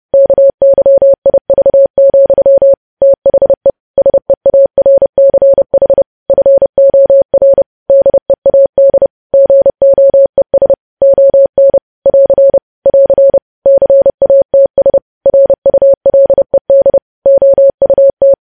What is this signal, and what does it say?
Morse code, heard as a keyed tone, about 30 wpm, 550 Hz. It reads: KYIV, THE SEARCH FOR DEAD GOES ON <AR> <AR> CATS RULED OUT
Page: news headlines in morse code at 30 wpm